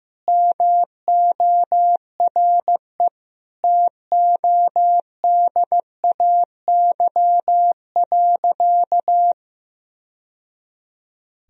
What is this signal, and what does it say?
Morse code, heard as a keyed tone, about 16 wpm, 700 Hz. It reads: MORE TODAY.
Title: Sentences from Top 1000 Words - Review Part 4 - 15wpm